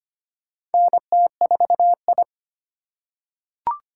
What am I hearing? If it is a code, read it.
Morse code, heard as a keyed tone, about 25 wpm, 700 Hz. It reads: NT4I
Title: USA Call Signs - 4 Characters - 3x Character Spacing - 25wpm